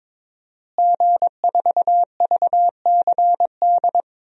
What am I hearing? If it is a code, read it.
Morse code, heard as a keyed tone, about 21 wpm, 700 Hz. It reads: G4VCD
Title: DXCC-100 Call Signs - 5 Characters - Repeated 2x - 22wpm